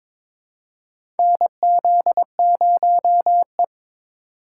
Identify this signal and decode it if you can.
Morse code, heard as a keyed tone, about 22 wpm, 700 Hz. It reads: NZ0E
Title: USA Call Signs - 4 Characters - 4x Character Spacing - 22wpm